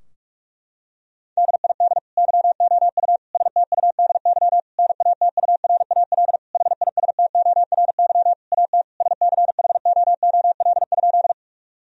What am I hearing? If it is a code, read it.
Morse code, heard as a keyed tone, about 44 wpm, 700 Hz. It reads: DID YOU STUDY NATURAL HISTORY AT SCHOOL?
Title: Sentences from Top 1000 Words - Review Part 4 - 45wpm